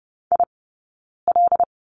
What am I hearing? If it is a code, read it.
Morse code, heard as a keyed tone, about 30 wpm, 700 Hz. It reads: I L
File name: Lesson 18d - Letter m - Two Characters 30wpm